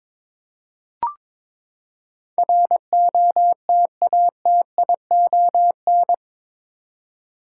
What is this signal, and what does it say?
Morse code, heard as a keyed tone, about 22 wpm, 700 Hz. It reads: ROTATION
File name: practice / 22wpm / Eight Letter Words - 22wpm